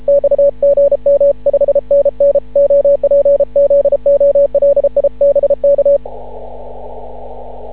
Morse code, heard as a keyed tone, about 32 wpm, 560 Hz. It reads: X GM5NN OPZOLIBK
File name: CW / ha6oi 2